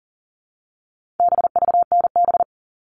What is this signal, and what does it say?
Morse code, heard as a keyed tone, about 40 wpm, 700 Hz. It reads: BVNB
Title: All Cut Numbers - 4-Digit 40wpm